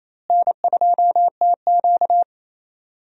Morse code, heard as a keyed tone, about 28 wpm, 700 Hz. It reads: N2TQ
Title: USA Call Signs - 4 Characters - Repeated 2x - 28wpm